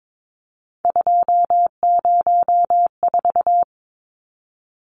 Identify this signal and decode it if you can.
Morse code, heard as a keyed tone, about 22 wpm, 700 Hz. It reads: 204
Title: Numbers - 3-Digit 22wpm